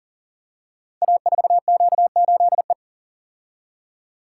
Morse code, heard as a keyed tone, about 38 wpm, 690 Hz. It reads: A4Q8E
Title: Code Groups - 5 Letter:Numbers - 40wpm